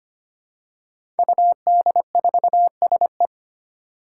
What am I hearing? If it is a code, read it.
Morse code, heard as a keyed tone, about 25 wpm, 700 Hz. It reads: UD4SE
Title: Five Challenging Letter-Number Combinations 25wpm